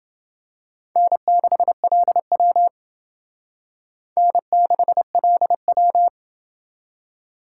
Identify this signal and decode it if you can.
Morse code, heard as a keyed tone, about 28 wpm, 700 Hz. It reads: N6LW N6LW